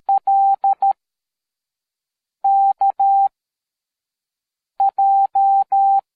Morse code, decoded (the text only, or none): L K J